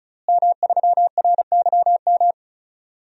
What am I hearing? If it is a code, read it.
Morse code, heard as a keyed tone, about 36 wpm, 700 Hz. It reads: M3RYM
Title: DXCC-100 Call Signs - 5 Characters - Repeated 2x - 35wpm